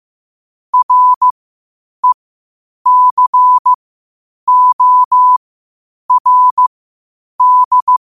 Morse code, heard as a keyed tone, about 15 wpm, spaced as for 8 wpm, 1000 Hz. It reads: RECORD
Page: 8 wpm morse code quotes